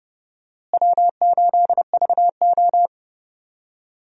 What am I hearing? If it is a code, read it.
Morse code, heard as a keyed tone, about 32 wpm, 700 Hz. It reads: W8VO
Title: USA Call Signs - 4 Characters - Repeated 2x - 30wpm